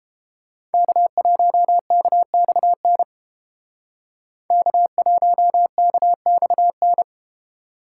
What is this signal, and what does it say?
Morse code, heard as a keyed tone, about 32 wpm, 700 Hz. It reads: K1KXN K1KXN